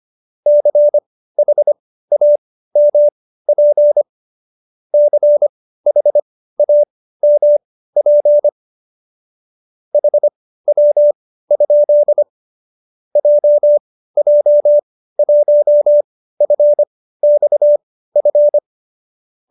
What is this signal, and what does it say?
Morse code, heard as a keyed tone, about 25 wpm, spaced as for 15 wpm, 590 Hz. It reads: CHAMP CHAMP HW? JJ1FXF